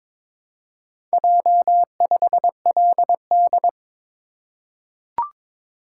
Morse code, heard as a keyed tone, about 22 wpm, 700 Hz. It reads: J5LD
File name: practice / 22wpm / Four Letter-Number Combinations - 22wpm